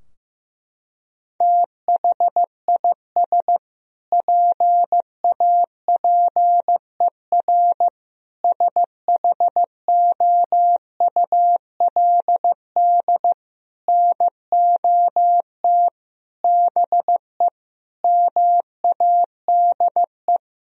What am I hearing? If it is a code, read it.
Morse code, heard as a keyed tone, about 15 wpm, 700 Hz. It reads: THIS PAPER SHOULD NOT BE MADE
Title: Sentences from Top 1000 Words - Review Part 3 - 15wpm